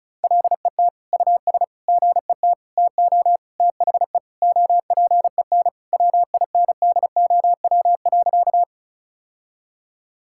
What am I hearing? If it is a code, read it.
Morse code, heard as a keyed tone, about 36 wpm, 700 Hz. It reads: LET US GET TO THE OPEN WINDOW.
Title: Sentences from Top 1600 Words 35wpm